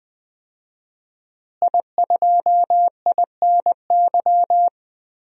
Morse code, heard as a keyed tone, about 21 wpm, 700 Hz. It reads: I2INY
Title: DXCC-100 Call Signs - 5 Characters - Repeated 3x - 20wpm